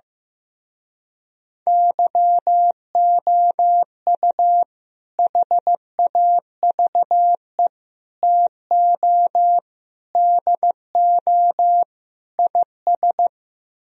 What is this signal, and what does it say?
Morse code, heard as a keyed tone, about 15 wpm, 700 Hz. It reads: YOU HAVE TO DO IS
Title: Speed-Racing - Sets of 5 Words 15wpm